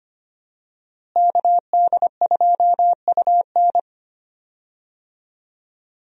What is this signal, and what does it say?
Morse code, heard as a keyed tone, about 25 wpm, 700 Hz. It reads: KD2UN
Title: USA Call Signs - 5 Characters - Repeated 3x - 25wpm